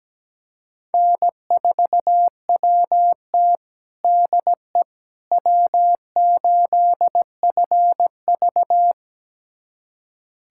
Morse code, heard as a keyed tone, about 17 wpm, 700 Hz. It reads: N4WT DE W8FV
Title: USA Callsign DE Callsign - 4 Characters - 17wpm